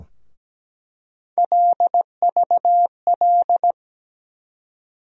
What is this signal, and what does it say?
Morse code, heard as a keyed tone, about 17 wpm, 700 Hz. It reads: LVL
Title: USA Call Sign Postfixes - 3 Letters 17wpm